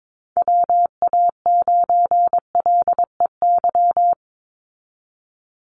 Morse code, heard as a keyed tone, about 22 wpm, 700 Hz. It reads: WA9LEY
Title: Call Signs Repeated 2x - 22wpm